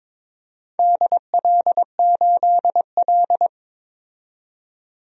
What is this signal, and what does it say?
Morse code, heard as a keyed tone, about 22 wpm, 700 Hz. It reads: DL8L
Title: DXCC-100 Call Signs - 4 Characters - 22wpm